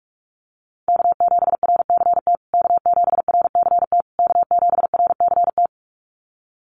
Morse code, heard as a keyed tone, about 44 wpm, 700 Hz. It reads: K7RCT K7RCT K7RCT
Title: USA Call Signs Repeated 3x 45wpm